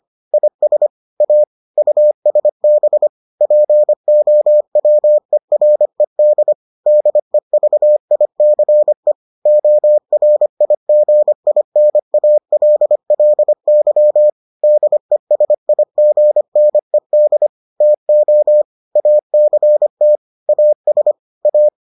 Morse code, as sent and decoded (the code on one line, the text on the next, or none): .. ... / .- / ..- ... -... / .--. --- .-- . .-. -.. / -.. ...- .. -.-. / --- .-. .. --. .. -. .- .-.. .-.. -.-- / -.. ... .. --. -. -.. / - --- / .- -.-. - / .- ... / .-
IS A USB POWERED DEVICE ORIGINALLY DESIGNED TO ACT AS A